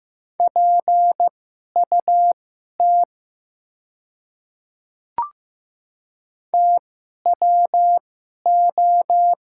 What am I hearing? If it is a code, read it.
Morse code, heard as a keyed tone, about 15 wpm, spaced as for 11 wpm, 700 Hz. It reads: PUT TWO
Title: 2x Character Spacing - Top 300 Words - 15wpm